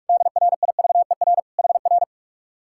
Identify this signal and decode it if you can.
Morse code, heard as a keyed tone, about 59 wpm, 700 Hz. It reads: DRIVER HR